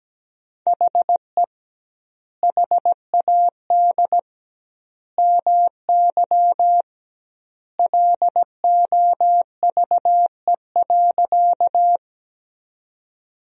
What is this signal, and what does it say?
Morse code, heard as a keyed tone, about 17 wpm, 700 Hz. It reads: HE HAD MY LOVE.
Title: 2x Word-Spacing - Sentences from Top 400 Words - 17wpm